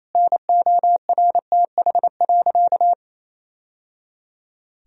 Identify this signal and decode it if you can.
Morse code, heard as a keyed tone, about 28 wpm, 700 Hz. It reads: NORTH.
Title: Sentences from Top 1000 Words - Review Part 1 - 28wpm